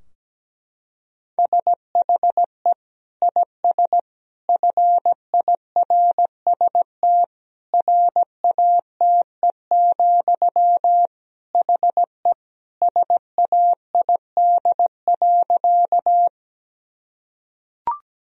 Morse code, heard as a keyed tone, about 17 wpm, 700 Hz. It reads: SHE IS FIRST RATE, HE SAID.
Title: Sentences from Top 1000 Words - Review Part 4 - 17wpm